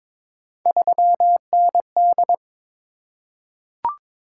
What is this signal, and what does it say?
Morse code, heard as a keyed tone, about 22 wpm, 700 Hz. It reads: 3ND
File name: practice / 22wpm / Three Letter-Number Combinations - 22wpm